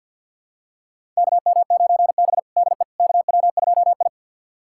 Morse code, heard as a keyed tone, about 53 wpm, 700 Hz. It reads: KM9B DE KW2I
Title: USA Callsign DE Callsign - 4 Characters - 50wpm